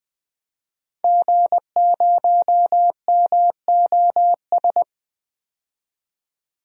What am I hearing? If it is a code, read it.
Morse code, heard as a keyed tone, about 20 wpm, 700 Hz. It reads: G0MOS
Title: DXCC-100 Call Signs - 5 Characters - 20wpm